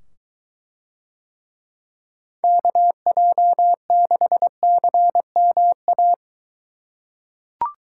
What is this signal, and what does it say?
Morse code, heard as a keyed tone, about 23 wpm, 700 Hz. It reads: KJ6CMA